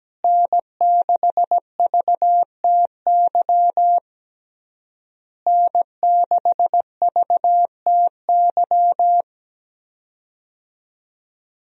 Morse code, heard as a keyed tone, about 17 wpm, 700 Hz. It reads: N6VTY N6VTY
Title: USA Call Signs - 5 Characters - Repeated 3x - 17wpm